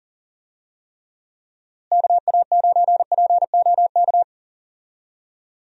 Morse code, heard as a keyed tone, about 40 wpm, 700 Hz. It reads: KA9POK
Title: USA Call Signs - 6 Characters - Repeated 3x - 40wpm